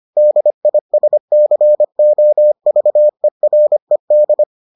Morse code, heard as a keyed tone, about 25 wpm, 600 Hz. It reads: DISCOVERED